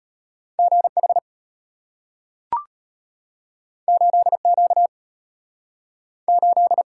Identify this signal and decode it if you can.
Morse code, heard as a keyed tone, about 34 wpm, 700 Hz. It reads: GH 8Q 8